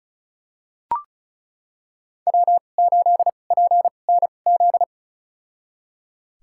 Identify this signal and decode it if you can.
Morse code, heard as a keyed tone, about 34 wpm, spaced as for 25 wpm, 700 Hz. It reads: W8PNZ
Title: USA Call Signs - 5 Characters - 2x Character Spacing - 35wpm